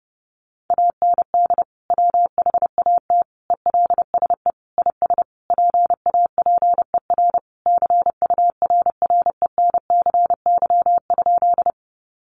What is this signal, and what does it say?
Morse code, heard as a keyed tone, about 30 wpm, 700 Hz. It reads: AND WHAT ELSE IS PAPER CURRENCY?
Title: Sentences from Top 1500 Words 30wpm